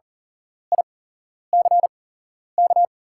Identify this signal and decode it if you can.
Morse code, heard as a keyed tone, about 40 wpm, spaced as for 11 wpm, 690 Hz. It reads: ICK
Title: 8x Character Spacing - Top 500 Words - 40wpm